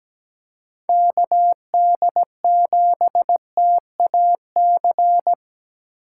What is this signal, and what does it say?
Morse code, heard as a keyed tone, about 17 wpm, 700 Hz. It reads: KD7TAC
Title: USA Call Signs - 6 Characters - 2x Character Spacing - 17wpm